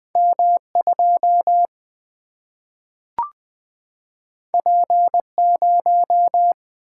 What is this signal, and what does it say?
Morse code, heard as a keyed tone, about 20 wpm, 700 Hz. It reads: M 2 P 0
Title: Single-Letter Single-Number 20wpm